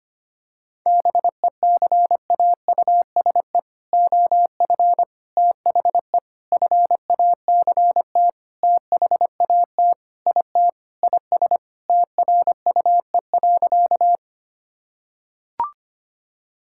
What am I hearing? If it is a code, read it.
Morse code, heard as a keyed tone, about 25 wpm, 700 Hz. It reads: BECAUSE OF THE FACT THAT IT IS TRUE.
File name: practice / 25wpm / Sentences with 4-Word Phrases - 25wpm